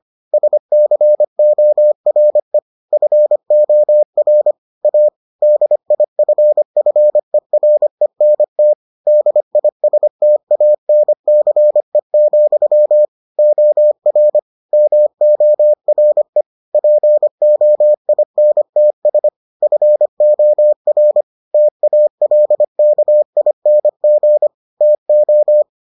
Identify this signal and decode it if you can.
Morse code, heard as a keyed tone, about 25 wpm, 600 Hz. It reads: SCORE FOR A DIFFERENT DISTANCE, OR MORE POINTS FOR TALKING TO